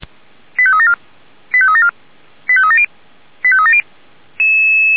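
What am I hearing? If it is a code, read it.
Five-tone selective call (ZVEI), heard as quick runs of five tones, tones of about 70 ms. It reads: 85374 85374 85370 85370
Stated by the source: ZVEI five-tone call received by SDR, from feuerwehr-orlamuende (fire brigade Orlamünde).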